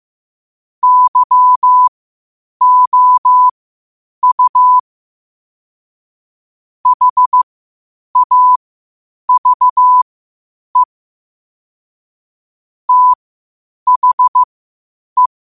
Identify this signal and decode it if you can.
Morse code, heard as a keyed tone, about 15 wpm, spaced as for 8 wpm, 1000 Hz. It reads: YOU HAVE THE
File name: qotdfiles / 8wpm